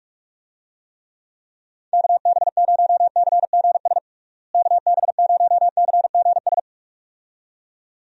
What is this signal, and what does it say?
Morse code, heard as a keyed tone, about 42 wpm, 700 Hz. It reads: KB0CGS KB0CGS
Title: USA Call Signs - 6 Characters - Repeated 2x - 45wpm